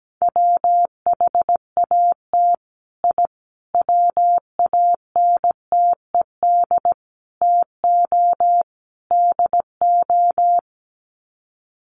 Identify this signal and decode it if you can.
Morse code, heard as a keyed tone, about 17 wpm, 700 Hz. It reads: WHAT I WANTED TO DO
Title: Sets of 5 Words Encore2 17wpm